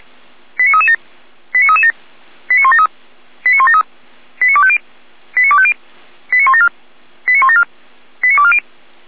Five-tone selective call (ZVEI), heard as quick runs of five tones, tones of about 70 ms. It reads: 89337 89337 89173 89173 89250 89250 89174 89174 89240
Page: ZVEI